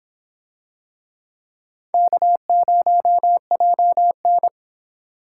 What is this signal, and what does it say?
Morse code, heard as a keyed tone, about 25 wpm, 700 Hz. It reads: K0JN